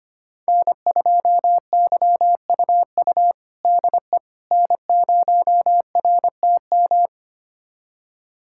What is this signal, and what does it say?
Morse code, heard as a keyed tone, about 25 wpm, 700 Hz. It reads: N2YUU DE N0RTM
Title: USA Callsign DE Callsign - 5 Characters - 25wpm